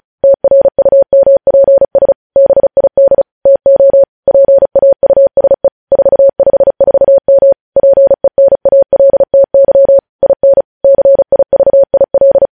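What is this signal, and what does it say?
Morse code, heard as a keyed tone, about 36 wpm, 550 Hz. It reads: TRUMPS BID TO PAUSE 454M PENALTY IN CIVIL
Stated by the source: News headlines sent in Morse code at 35 WPM.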